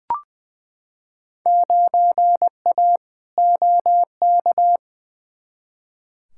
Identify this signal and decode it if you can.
Morse code, heard as a keyed tone, about 20 wpm, 700 Hz. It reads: 9A OK
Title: Field Day Exchanges Encore - 20wpm